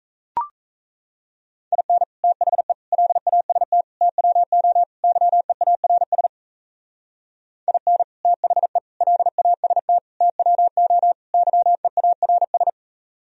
Morse code, heard as a keyed tone, about 35 wpm, 700 Hz. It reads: IN THE LAST TWO YEARS IN THE LAST TWO YEARS